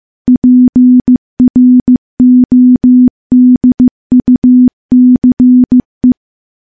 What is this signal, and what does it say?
Morse code, heard as a keyed tone, about 15 wpm, 250 Hz. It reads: PRODUCE
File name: morse-training-15WPM